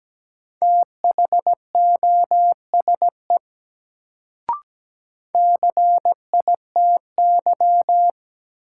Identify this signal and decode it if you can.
Morse code, heard as a keyed tone, about 17 wpm, 700 Hz. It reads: THOSE CITY
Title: Sentences from Top 300 Words - New Word Drill - 17wpm